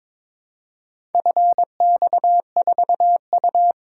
Morse code, heard as a keyed tone, about 22 wpm, 700 Hz. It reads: FX4U
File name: practice / 22wpm / Four Letter-Number Combinations - 22wpm